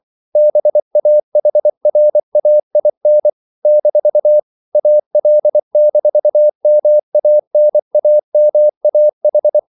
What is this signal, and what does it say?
Morse code, heard as a keyed tone, about 24 wpm, 600 Hz. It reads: BAHRAIN - AL-MANAMAH
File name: CW / Practice / NationalCapitals24